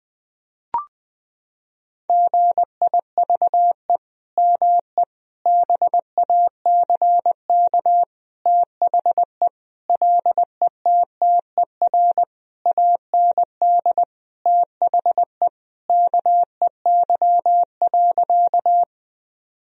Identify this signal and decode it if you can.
Morse code, heard as a keyed tone, about 20 wpm, 700 Hz. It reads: GIVE ME BACK THE LETTER AND THE KEY.
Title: Sentences from Top 1000 Words - Review Part 4 - 20wpm